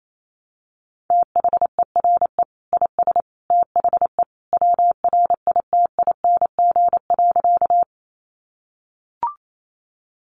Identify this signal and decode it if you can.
Morse code, heard as a keyed tone, about 28 wpm, 700 Hz. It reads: THERE IS THE WRITING.
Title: Sentences from Top 800 Words 28wpm